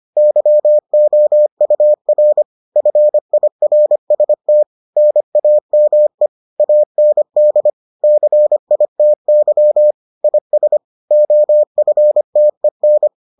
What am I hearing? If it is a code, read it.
Morse code, heard as a keyed tone, about 25 wpm, 600 Hz. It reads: YOUR FIRST NAME AND CITY IS OFTEN